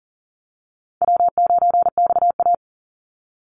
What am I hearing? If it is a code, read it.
Morse code, heard as a keyed tone, about 38 wpm, 700 Hz. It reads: W9XA
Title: Easy Call Signs Repeated 3x - 40wpm